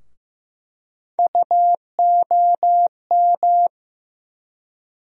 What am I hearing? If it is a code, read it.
Morse code, heard as a keyed tone, about 15 wpm, 700 Hz. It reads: UOM
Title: USA Call Sign Postfixes - 3 Letters 15wpm